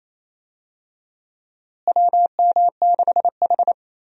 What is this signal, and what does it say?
Morse code, heard as a keyed tone, about 28 wpm, 700 Hz. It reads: WM6H
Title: USA Call Signs - 4 Characters - Repeated 2x - 28wpm